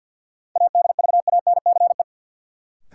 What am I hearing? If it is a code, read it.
Morse code, heard as a keyed tone, about 51 wpm, 700 Hz. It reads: ADVANCE
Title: Sentences from Top 1400 Words - New Word Drill - 50wpm